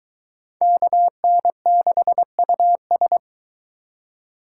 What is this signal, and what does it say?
Morse code, heard as a keyed tone, about 22 wpm, 700 Hz. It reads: KN6US